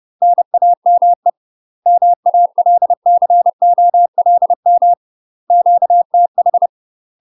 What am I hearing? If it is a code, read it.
Morse code, heard as a keyed tone, about 30 wpm, 700 Hz. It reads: NAME MALCOLM QTH
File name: UK MASTER QSOs TRAINING/30WPM